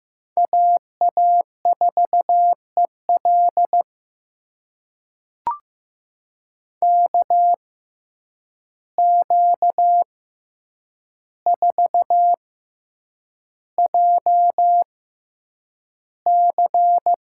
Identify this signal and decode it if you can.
Morse code, heard as a keyed tone, about 15 wpm, spaced as for 8 wpm, 700 Hz. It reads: AA4EL KQ4JC
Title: USA Call Signs - 5 Characters - 6x Character Spacing - 15wpm